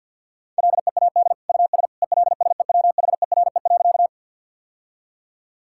Morse code, heard as a keyed tone, about 50 wpm, 700 Hz. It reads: LEAD US ELSEWHERE.
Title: Sentences from Top 1600 Words 50wpm